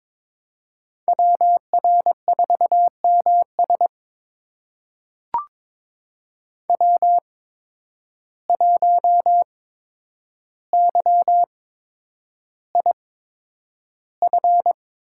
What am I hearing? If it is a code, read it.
Morse code, heard as a keyed tone, about 22 wpm, spaced as for 9 wpm, 700 Hz. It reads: WR4MS W1YIF
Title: USA Call Signs - 5 Characters - 8x Character Spacing - 22wpm